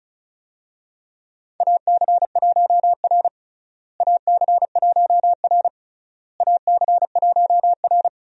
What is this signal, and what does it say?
Morse code, heard as a keyed tone, about 36 wpm, 700 Hz. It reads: AC1R AC1R AC1R
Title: USA Call Signs - 4 Characters - Repeated 3x - 35wpm